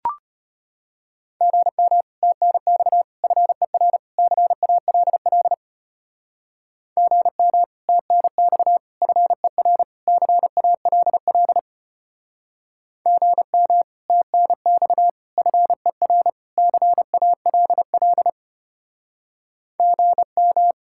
Morse code, heard as a keyed tone, about 32 wpm, 700 Hz. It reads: GM TNX FER CALL GM TNX FER CALL GM TNX FER CALL GM